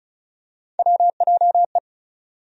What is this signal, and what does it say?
Morse code, heard as a keyed tone, about 36 wpm, 700 Hz. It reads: WJE
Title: USA Call Sign Postfixes - 3 Letters 35wpm